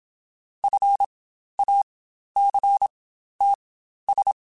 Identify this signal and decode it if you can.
Morse code, heard as a keyed tone, about 26 wpm, spaced as for 12 wpm, 800 Hz. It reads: FACTS
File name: Quotes Spaced / Quotes Spaced-mp3s / IgnoringFacts sp @25